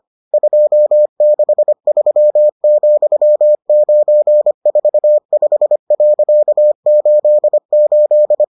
Morse code, heard as a keyed tone, about 25 wpm, 600 Hz. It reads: 263,945.88